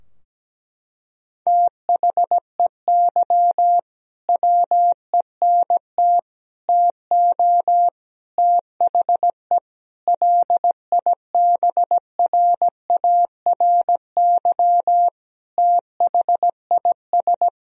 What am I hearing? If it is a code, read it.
Morse code, heard as a keyed tone, about 17 wpm, 700 Hz. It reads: THEY WENT TO THE LIBRARY THIS
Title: Sentences from Top 1000 Words - Review Part 4 - 17wpm